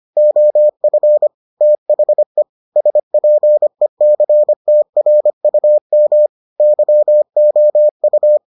Morse code, heard as a keyed tone, about 25 wpm, 600 Hz. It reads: OF THE SPECTRUM YOU